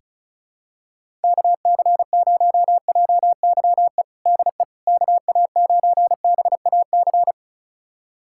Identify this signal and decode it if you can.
Morse code, heard as a keyed tone, about 36 wpm, 700 Hz. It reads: KC0JYE DE KA9BAC
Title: USA Callsign DE Callsign - 6 Characters - 35wpm